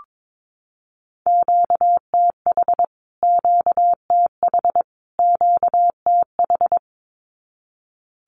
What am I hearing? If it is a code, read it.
Morse code, heard as a keyed tone, about 22 wpm, 700 Hz. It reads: QTH QTH QTH
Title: Q-Codes Repeated 3x 22wpm